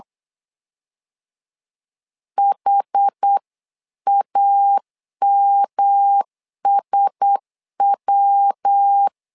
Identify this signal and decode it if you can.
Morse code, heard as a keyed tone, about 8 wpm, 800 Hz. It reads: HAMSW
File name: LEZIONI / 4 Q1S3 8wpm 3037